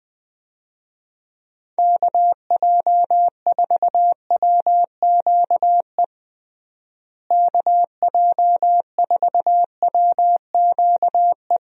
Morse code, heard as a keyed tone, about 20 wpm, 700 Hz. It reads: KJ4WQE KJ4WQE
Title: USA Call Signs - 6 Characters - Repeated 2x - 20wpm